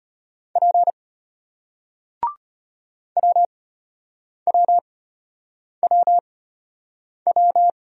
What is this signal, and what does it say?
Morse code, heard as a keyed tone, about 34 wpm, spaced as for 8 wpm, 700 Hz. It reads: P WWWW